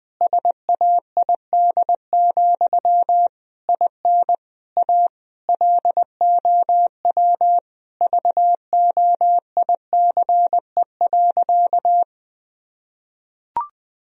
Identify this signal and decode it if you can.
Morse code, heard as a keyed tone, about 20 wpm, 700 Hz. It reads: SAID, IN A LOW VOICE.
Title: Sentences from Top 1000 Words - Review Part 1 - 20wpm